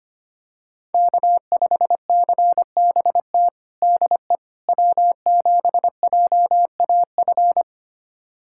Morse code, heard as a keyed tone, about 25 wpm, 700 Hz. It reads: K5CBT DE W7JAF
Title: USA Callsign DE Callsign - 5 Characters - 25wpm